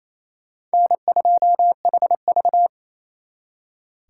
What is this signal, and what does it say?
Morse code, heard as a keyed tone, about 30 wpm, 700 Hz. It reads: N2HV
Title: Speed Racing - USA 1x2 Call Signs 28wpm